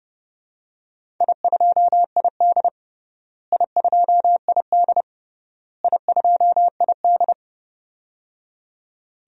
Morse code, heard as a keyed tone, about 30 wpm, 700 Hz. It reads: I2ID I2ID I2ID
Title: DXCC-100 Call Signs - 4 Characters - Repeated 3x - 30wpm